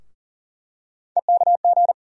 Morse code, heard as a keyed tone, about 40 wpm, 700 Hz. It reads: EKG
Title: USA Call Sign Postfixes - 3 Letters 40wpm